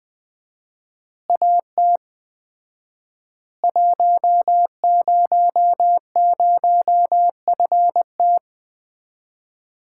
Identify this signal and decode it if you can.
Morse code, heard as a keyed tone, about 20 wpm, 700 Hz. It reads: AT 100FT
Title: Common QSO Phrases - 4x Word-Spacing - 20wpm